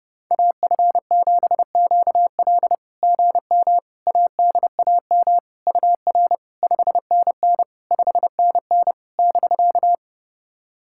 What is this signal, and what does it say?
Morse code, heard as a keyed tone, about 30 wpm, 690 Hz. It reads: AF7QL GM ADAM UR 5NN 5NN <BK>
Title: SOTA QSO Practice 30wpm